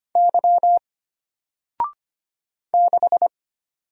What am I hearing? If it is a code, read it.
Morse code, heard as a keyed tone, about 25 wpm, 700 Hz. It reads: Y 6
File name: Single-Letter-Number-25wpm